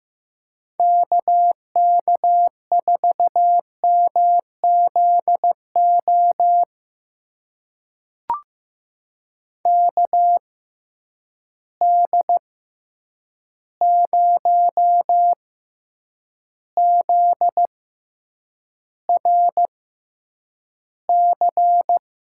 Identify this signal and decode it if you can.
Morse code, heard as a keyed tone, about 15 wpm, spaced as for 8 wpm, 700 Hz. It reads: KK4MZO KD0ZRC
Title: USA Call Signs - 6 Characters - 6x Character Spacing - 15wpm